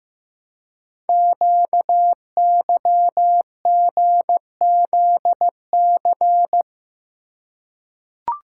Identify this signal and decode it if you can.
Morse code, heard as a keyed tone, about 15 wpm, 700 Hz. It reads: QYGZC
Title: Code Groups - 5 Letters - 15wpm